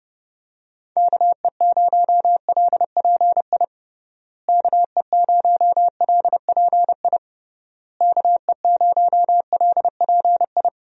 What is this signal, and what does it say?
Morse code, heard as a keyed tone, about 30 wpm, 700 Hz. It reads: KE0LPI KE0LPI KE0LPI
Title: USA Call Signs - 6 Characters - Repeated 3x - 30wpm